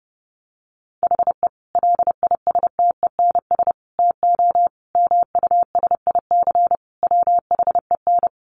Morse code, heard as a keyed tone, about 30 wpm, 700 Hz. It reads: HE LISTENS TO MUSIC WHEN
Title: Additional Easy Phrases 30wpm